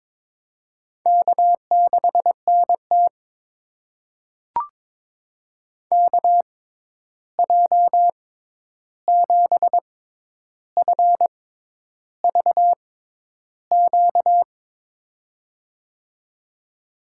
Morse code, heard as a keyed tone, about 22 wpm, spaced as for 10 wpm, 700 Hz. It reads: K6NT KJ7FVQ
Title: USA Call Signs - 6x Character Spacing - 22wpm